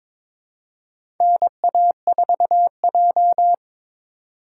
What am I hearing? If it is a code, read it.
Morse code, heard as a keyed tone, about 22 wpm, 700 Hz. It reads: NA4J
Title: USA Call Signs - 4 Characters - 4x Character Spacing - 22wpm